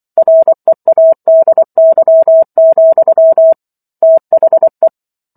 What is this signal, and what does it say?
Morse code, heard as a keyed tone, about 24 wpm, 650 Hz. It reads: READY, THE